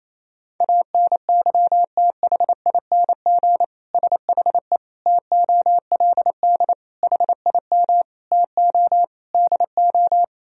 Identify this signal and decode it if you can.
Morse code, heard as a keyed tone, about 28 wpm, 700 Hz. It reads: ANYTHING SHE TOLD HIM TO DO
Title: Sentences from Top 1000 Words - Review Part 3 - 28wpm